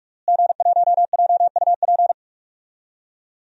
Morse code, heard as a keyed tone, about 46 wpm, 700 Hz. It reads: G1JUP
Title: DXCC-100 Call Signs - 5 Characters - Repeated 3x - 45wpm